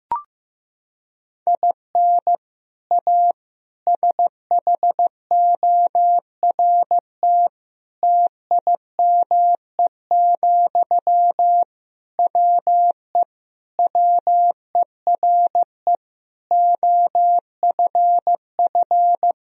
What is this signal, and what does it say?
Morse code, heard as a keyed tone, about 15 wpm, 700 Hz. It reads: IN A SHORT TIME, WE WERE OFF
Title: Sentences from Top 1000 Words - Review Part 2 - 15wpm